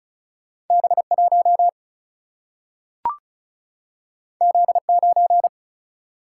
Morse code, heard as a keyed tone, about 36 wpm, 700 Hz. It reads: B1 Z9
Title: Single-Letter Single-Number 35wpm